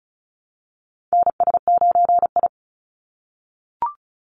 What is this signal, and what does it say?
Morse code, heard as a keyed tone, about 36 wpm, 700 Hz. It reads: NS9I
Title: Call Signs - 35wpm